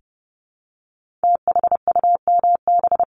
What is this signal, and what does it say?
Morse code, heard as a keyed tone, about 30 wpm, 700 Hz. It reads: THUMB
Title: Multi-letter Phonemes 30wpm